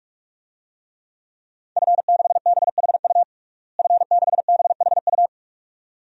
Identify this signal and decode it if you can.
Morse code, heard as a keyed tone, about 42 wpm, 700 Hz. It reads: F6BHU F6BHU
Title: DXCC-100 Call Signs - 5 Characters - Repeated 2x - 45wpm